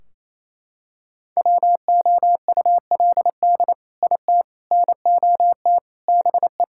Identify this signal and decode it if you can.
Morse code, heard as a keyed tone, about 28 wpm, 700 Hz. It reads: WOULD IT NOT BE
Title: Sentences from Top 1600 Words 28wpm